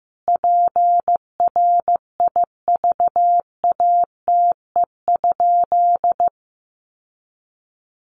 Morse code, heard as a keyed tone, about 15 wpm, 700 Hz. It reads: PRIVATE?
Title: Sentences from Top 1000 Words 15wpm